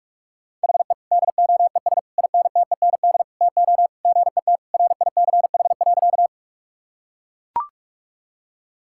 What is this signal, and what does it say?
Morse code, heard as a keyed tone, about 41 wpm, 700 Hz. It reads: HE DOES INTEND TO GET RICH.
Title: Sentences from Top 1600 Words 45wpm